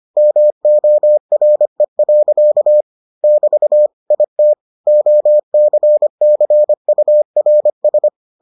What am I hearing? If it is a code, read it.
Morse code, heard as a keyed tone, about 25 wpm, 600 Hz. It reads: MORE. <BT> IT OCCURS